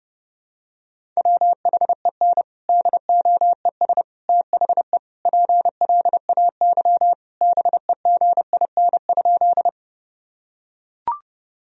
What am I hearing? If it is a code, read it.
Morse code, heard as a keyed tone, about 30 wpm, 700 Hz. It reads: WHEN DOES THE PLAY BEGIN?
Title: Sentences from Top 1000 Words - Review Part 1 - 30wpm